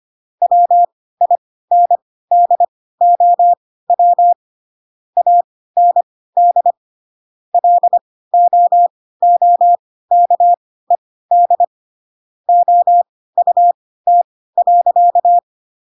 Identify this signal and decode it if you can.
Morse code, heard as a keyed tone, about 25 wpm, spaced as for 16 wpm, 700 Hz. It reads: WINDOW AND LOOKED OUT.
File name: Pinocchio / Pinocchio ch6 10 25 ik1hge